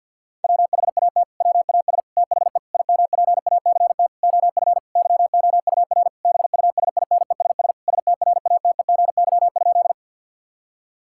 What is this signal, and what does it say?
Morse code, heard as a keyed tone, about 50 wpm, 700 Hz. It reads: WHAT WAS THE IMPACT OF YOUR BUSINESS STRATEGY?
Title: Sentences from Top 1000 Words - Review Part 6 - 50wpm